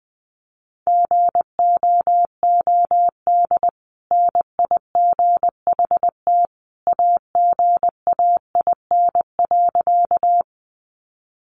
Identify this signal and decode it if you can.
Morse code, heard as a keyed tone, about 20 wpm, 700 Hz. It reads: GOOD NIGHT AGAIN.
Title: Sentences from Top 300 Words 20wpm